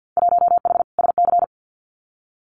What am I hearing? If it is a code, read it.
Morse code, heard as a keyed tone, about 50 wpm, 700 Hz. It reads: 1H SC
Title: Winter Field Day Exchanges 50wpm